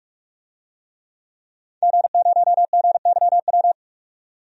Morse code, heard as a keyed tone, about 46 wpm, 700 Hz. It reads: G0GYW
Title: DXCC-100 Call Signs - 5 Characters - 45wpm